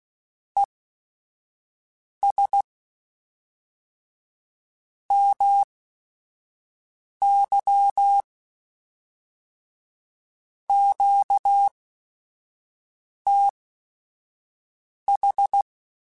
Morse code, heard as a keyed tone, about 16 wpm, spaced as for 5 wpm, 800 Hz. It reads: ES MY QTH
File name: QSOs / QSOs-mp3s / WD0BGP 1@5